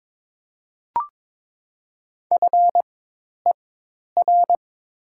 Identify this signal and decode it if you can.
Morse code, heard as a keyed tone, about 22 wpm, spaced as for 10 wpm, 700 Hz. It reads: FER
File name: Top 20 QSO Elements - 4x Character Spacing - 22wpm